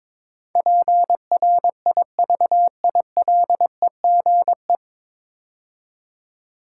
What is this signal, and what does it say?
Morse code, heard as a keyed tone, about 22 wpm, 700 Hz. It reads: PRIVILEGE
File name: 9 Letter Words - First 3 Letters Given - 22wpm